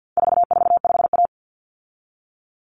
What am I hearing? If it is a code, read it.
Morse code, heard as a keyed tone, about 48 wpm, 700 Hz. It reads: 445A
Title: Common Cut Numbers - 4-Digit 50wpm